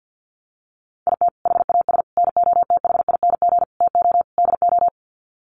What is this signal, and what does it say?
Morse code, heard as a keyed tone, about 49 wpm, 700 Hz. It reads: IT HAS NOTHING TO DO
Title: Sets of 5 Words Encore 50wpm